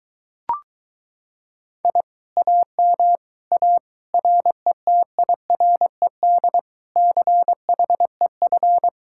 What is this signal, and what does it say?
Morse code, heard as a keyed tone, about 22 wpm, 700 Hz. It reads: I AM A RETIRED CHEF